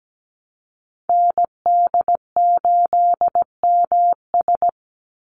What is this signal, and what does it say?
Morse code, heard as a keyed tone, about 17 wpm, 700 Hz. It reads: ND8MS
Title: Call Signs Repeated 2x - 17wpm